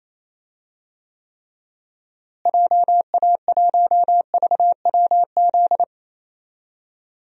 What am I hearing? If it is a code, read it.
Morse code, heard as a keyed tone, about 28 wpm, 700 Hz. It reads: JA1VWZ
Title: DXCC-100 Call Signs - 6 Characters - Repeated 3x - 28wpm